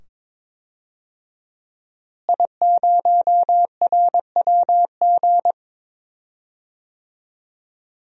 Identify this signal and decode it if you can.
Morse code, heard as a keyed tone, about 22 wpm, 700 Hz. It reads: I0RWG
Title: DXCC-100 Call Signs - 5 Characters - Repeated 2x - 22wpm